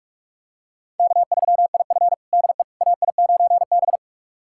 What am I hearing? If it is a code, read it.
Morse code, heard as a keyed tone, about 46 wpm, 700 Hz. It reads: K3IF DE AI9B
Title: USA Callsign DE Callsign - 4 Characters - 45wpm